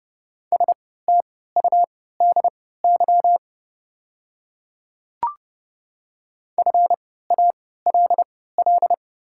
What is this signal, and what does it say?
Morse code, heard as a keyed tone, about 30 wpm, spaced as for 17 wpm, 700 Hz. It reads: STUDY FALL
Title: 3x Character Spacing - Top 500 Words - 30wpm